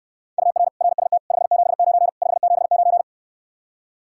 Morse code, heard as a keyed tone, about 69 wpm, 700 Hz. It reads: UR RST 579 579